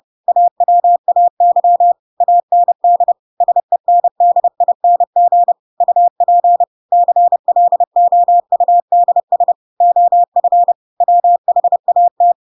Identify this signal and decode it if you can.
Morse code, heard as a keyed tone, about 30 wpm, 700 Hz. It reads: AWAY AND SENDING UP CLOUDS OF WHAT
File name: Pinocchio / Pinocchio ch3 30 30 ik1hge